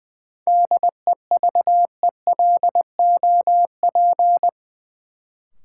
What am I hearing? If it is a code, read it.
Morse code, heard as a keyed tone, about 20 wpm, 700 Hz. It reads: DEVELOP
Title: Sentences from Top 1000 Words - Review Part 2 - New Word Drill - 20wpm